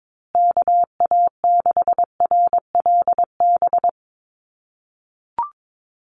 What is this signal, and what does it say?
Morse code, heard as a keyed tone, about 22 wpm, 700 Hz. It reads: KA6RLB
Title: USA Call Signs Repeated 3x 22wpm